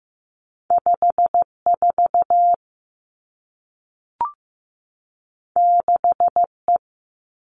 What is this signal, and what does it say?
Morse code, heard as a keyed tone, about 15 wpm, 700 Hz. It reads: 5 4 6 E
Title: Two Challenging Letter-Number Combinations 15wpm